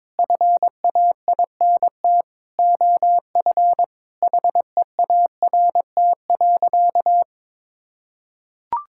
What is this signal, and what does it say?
Morse code, heard as a keyed tone, about 22 wpm, 700 Hz. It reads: FAINT OF HEART.
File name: Sentences with 7-Word Phrases - 22wpm